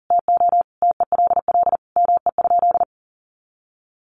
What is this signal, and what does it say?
Morse code, heard as a keyed tone, about 40 wpm, 700 Hz. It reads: TO TELL ME?
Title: Sentences from Top 600 Words 40wpm